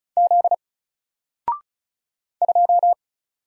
Morse code, heard as a keyed tone, about 36 wpm, 700 Hz. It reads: Z 2
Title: Single Letter-Number Alt 35wpm